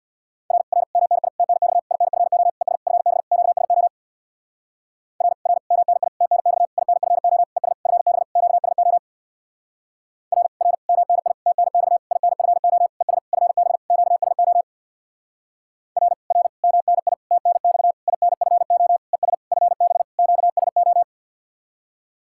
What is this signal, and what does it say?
Morse code, heard as a keyed tone, about 64 wpm, 700 Hz. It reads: R R MNI TNX INFO ES FB QSO R R MNI TNX INFO ES FB QSO R R MNI TNX INFO ES FB QSO R R MNI TNX INFO ES FB QSO